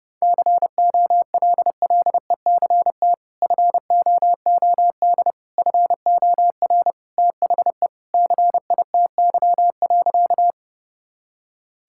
Morse code, heard as a keyed tone, about 32 wpm, 700 Hz. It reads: COLLECT FOOD FOR THE CITY.
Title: Sentences from Top 1600 Words 30wpm